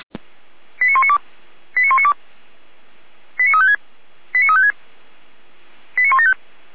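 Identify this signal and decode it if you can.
Five-tone selective call (ZVEI), heard as quick runs of five tones, tones of about 70 ms. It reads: 89192 89192 89356 89356 89175
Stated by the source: ZVEI five-tone selective call received on SDR.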